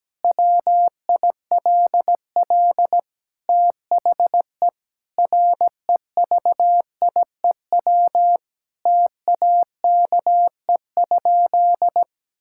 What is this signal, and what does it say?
Morse code, heard as a keyed tone, about 17 wpm, 700 Hz. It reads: WILL THE REVIEW TAKE?
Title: Sentences from Top 1000 Words - Review Part 6 - 17wpm